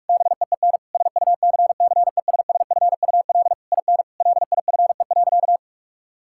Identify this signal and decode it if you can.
Morse code, heard as a keyed tone, about 46 wpm, 700 Hz. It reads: BEEN SUCCESSFUL IN LIFE.
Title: Sentences from Top 1000 Words - Review Part 6 - 45wpm